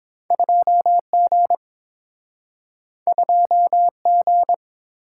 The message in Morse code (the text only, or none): 2G 2G